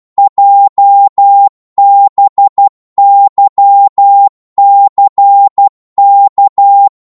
Morse code, heard as a keyed tone, about 12 wpm, 800 Hz. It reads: JBYCK
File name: Morse / ltr-12wpm